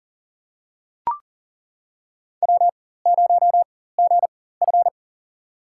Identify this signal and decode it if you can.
Morse code, heard as a keyed tone, about 40 wpm, spaced as for 19 wpm, 700 Hz. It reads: W0GF
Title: USA Call Signs - 4x Character Spacing - 40wpm